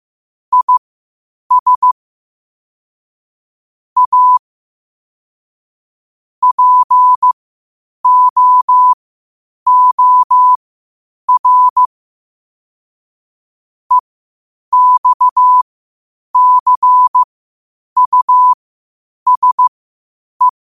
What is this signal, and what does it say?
Morse code, heard as a keyed tone, about 15 wpm, spaced as for 8 wpm, 1000 Hz. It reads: IS A POOR EXCUSE
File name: qotdfiles / 8wpm